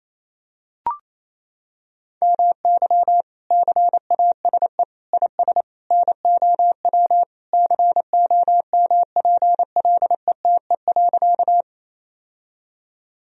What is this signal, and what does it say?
Morse code, heard as a keyed tone, about 28 wpm, 700 Hz. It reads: MY CASE IS NOW COMPLETE.
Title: Sentences from Top 1000 Words - Review Part 2 - 28wpm